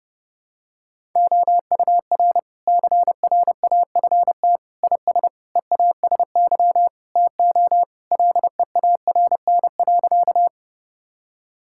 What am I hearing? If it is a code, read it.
Morse code, heard as a keyed tone, about 32 wpm, 700 Hz. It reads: OUR CRAFT IS EASY TO LEARN.
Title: Sentences from Top 1000 Words - Review Part 6 - 30wpm